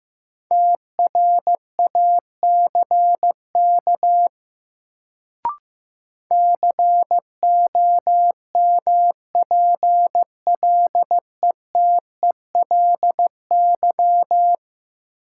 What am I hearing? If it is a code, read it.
Morse code, heard as a keyed tone, about 15 wpm, 700 Hz. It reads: TRACK COMPLETELY
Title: Sentences from Top 1100 Words - New Word Drill - 15wpm